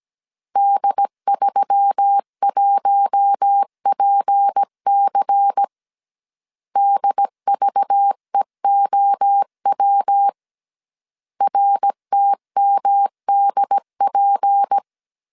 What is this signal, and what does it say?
Morse code, heard as a keyed tone, about 17 wpm, 800 Hz. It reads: D31PC DVEOW RTMDP
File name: 16wpm